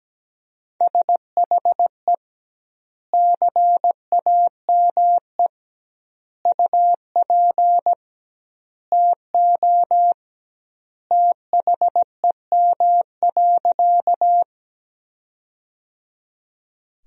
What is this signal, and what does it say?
Morse code, heard as a keyed tone, about 17 wpm, 700 Hz. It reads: SHE CAME UP TO THEM.
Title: 2x Word-Spacing - Sentences from Top 200 Words - 17wpm